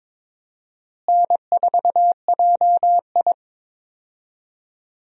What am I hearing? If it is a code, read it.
Morse code, heard as a keyed tone, about 22 wpm, 700 Hz. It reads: N4JI
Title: USA Call Signs - 4 Characters - Repeated 2x - 22wpm